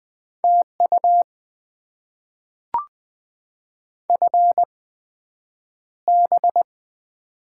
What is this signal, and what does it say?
Morse code, heard as a keyed tone, about 20 wpm, spaced as for 9 wpm, 700 Hz. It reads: TU FB
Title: Top 20 QSO Elements - 8x Character Spacing - 20wpm